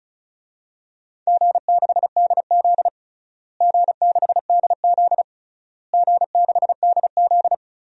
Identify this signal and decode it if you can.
Morse code, heard as a keyed tone, about 36 wpm, 700 Hz. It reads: G6DZ G6DZ G6DZ
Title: DXCC-100 Call Signs - 4 Characters - Repeated 3x - 35wpm